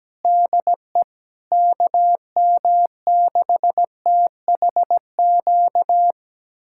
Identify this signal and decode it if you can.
Morse code, heard as a keyed tone, about 17 wpm, 700 Hz. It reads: DE KM6THQ
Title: USA Callsign DE Callsign - 6 Characters - 17wpm